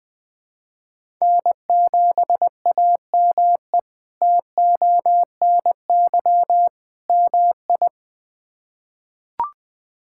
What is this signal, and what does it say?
Morse code, heard as a keyed tone, about 20 wpm, 700 Hz. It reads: N7AME TONY MI
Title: K1USN SST QSO Practice - 20wpm